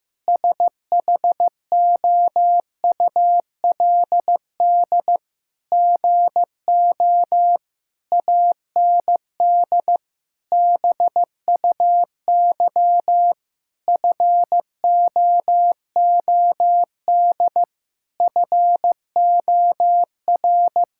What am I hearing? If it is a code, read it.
Morse code, heard as a keyed tone, about 15 wpm, 700 Hz. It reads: SHOULD GO AND BUY FOOD FOR
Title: Sentences from Top 1600 Words 15wpm